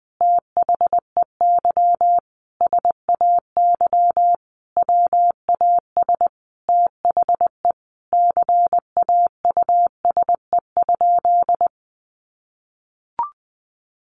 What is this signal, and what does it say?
Morse code, heard as a keyed tone, about 20 wpm, 700 Hz. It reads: THEY SAY WAS THE CAUSE?
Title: Sentences from Top 200 Words 20wpm